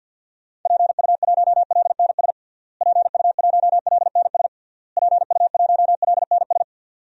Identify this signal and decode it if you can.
Morse code, heard as a keyed tone, about 52 wpm, 700 Hz. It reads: PU1LNS PU1LNS PU1LNS
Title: DXCC-100 Call Signs - 6 Characters - Repeated 3x - 50wpm